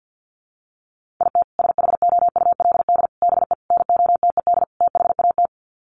Morse code, heard as a keyed tone, about 50 wpm, 700 Hz. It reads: IT SHOULD BE NOTED THAT
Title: Sets of 5 Words 50wpm